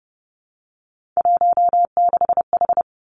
Morse code, heard as a keyed tone, about 30 wpm, 700 Hz. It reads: 16H
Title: Three Challenging Letter-Number Combinations 30wpm